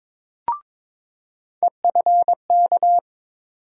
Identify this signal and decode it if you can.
Morse code, heard as a keyed tone, about 22 wpm, 700 Hz. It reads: EFK
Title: USA Call Sign Postfixes - 3 Letters 22wpm